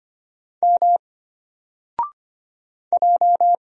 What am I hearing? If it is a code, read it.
Morse code, heard as a keyed tone, about 25 wpm, 700 Hz. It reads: M J
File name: Single-Letter-Number-25wpm